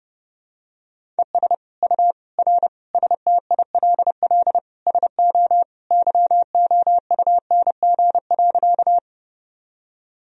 Morse code, heard as a keyed tone, about 30 wpm, 700 Hz. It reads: ES U R STILL SO YOUNG.
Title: Sentences from Top 400 Words and Abbreviations 30wpm